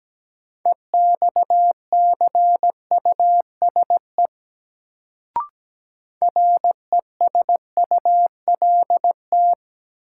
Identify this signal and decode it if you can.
Morse code, heard as a keyed tone, about 17 wpm, 700 Hz. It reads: EXCUSE RESULT
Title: Sentences from Top 700 Words - New Word Drill - 17wpm